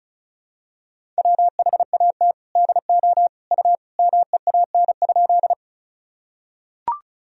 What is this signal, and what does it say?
Morse code, heard as a keyed tone, about 35 wpm, 700 Hz. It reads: WHAT DO U MEAN?
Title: Sentences from Top 200 Words and Abbreviations 35wpm